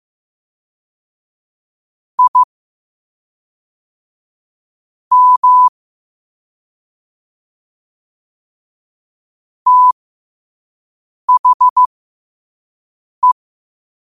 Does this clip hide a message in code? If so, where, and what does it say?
Yes: Morse code from 2.2 to 13.3 s, heard as a keyed tone, about 15 wpm, spaced as for 4 wpm, 1000 Hz. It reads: IM THE